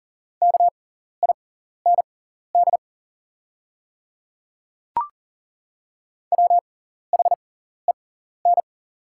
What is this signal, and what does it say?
Morse code, heard as a keyed tone, about 40 wpm, spaced as for 14 wpm, 700 Hz. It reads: KIND WHEN
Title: 6x Character Spacing - Top 500 Words - 40wpm